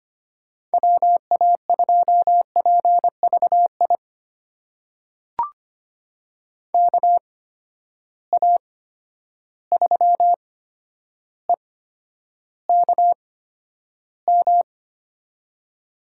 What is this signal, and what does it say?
Morse code, heard as a keyed tone, about 25 wpm, spaced as for 11 wpm, 700 Hz. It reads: WA2PVI KA3EKM
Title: USA Call Signs - 6 Characters - 8x Character Spacing - 25wpm